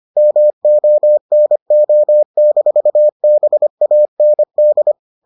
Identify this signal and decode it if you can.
Morse code, heard as a keyed tone, about 25 wpm, 600 Hz. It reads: MONO-BAND